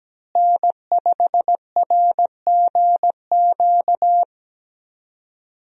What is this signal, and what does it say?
Morse code, heard as a keyed tone, about 17 wpm, 700 Hz. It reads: N5RGQ
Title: USA Call Signs - 5 Characters - Repeated 3x - 17wpm